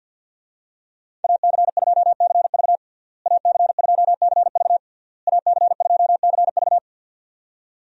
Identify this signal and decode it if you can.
Morse code, heard as a keyed tone, about 52 wpm, 700 Hz. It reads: AC2CV AC2CV AC2CV
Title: USA Call Signs - 5 Characters - Repeated 3x - 50wpm